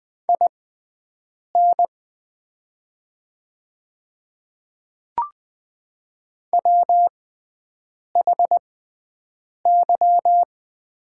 Morse code, heard as a keyed tone, about 20 wpm, spaced as for 7 wpm, 700 Hz. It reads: IN WHY
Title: 6x Character Spacing - Top 200 Words - 20wpm